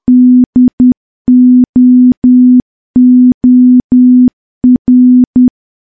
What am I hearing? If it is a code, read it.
Morse code, heard as a keyed tone, about 10 wpm, 250 Hz. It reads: DOOR
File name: morse / morse-training-10WPM